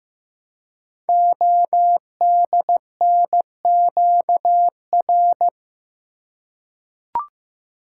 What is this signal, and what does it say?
Morse code, heard as a keyed tone, about 15 wpm, 700 Hz. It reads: ODNQR
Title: Code Groups - 5 Letters - 15wpm